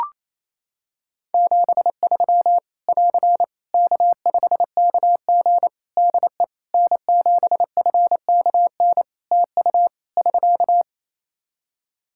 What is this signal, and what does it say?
Morse code, heard as a keyed tone, about 28 wpm, 700 Hz. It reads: 73 <AR> K5KG DE N7FKN TU <SK>
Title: K1USN SST QSO Practice 28wpm